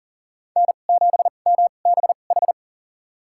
Morse code, heard as a keyed tone, about 40 wpm, spaced as for 29 wpm, 700 Hz. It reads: N7MBH
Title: USA Call Signs - 5 Characters - 2x Character Spacing - 40wpm